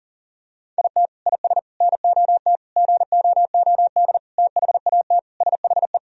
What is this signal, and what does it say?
Morse code, heard as a keyed tone, about 40 wpm, 700 Hz. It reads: IT IS NOT GOOD THAT SHE
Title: Sentences from Top 1000 Words - Review Part 4 - 40wpm